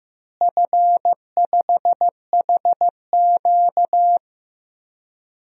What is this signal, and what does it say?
Morse code, heard as a keyed tone, about 15 wpm, 700 Hz. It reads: F5HQ
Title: DXCC-100 Call Signs - 4 Characters - Repeated 2x - 15wpm